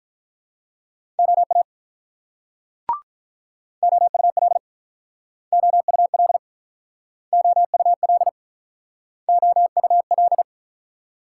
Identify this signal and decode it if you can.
Morse code, heard as a keyed tone, about 46 wpm, 700 Hz. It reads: OA OUL OUL OUL OUL